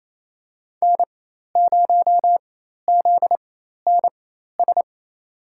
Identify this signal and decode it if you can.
Morse code, heard as a keyed tone, about 28 wpm, spaced as for 13 wpm, 700 Hz. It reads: N0ZNS
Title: USA Call Signs - 5 Characters - 4x Character Spacing - 28wpm